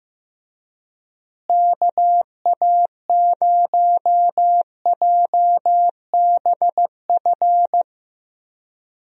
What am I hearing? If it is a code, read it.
Morse code, heard as a keyed tone, about 15 wpm, 700 Hz. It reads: KA0JBF
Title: USA Call Signs - 6 Characters - Repeated 3x - 15wpm